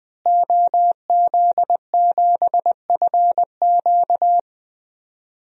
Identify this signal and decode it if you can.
Morse code, heard as a keyed tone, about 20 wpm, 700 Hz. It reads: OZ7FQ
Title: DXCC-100 Call Signs - 5 Characters - Repeated 2x - 20wpm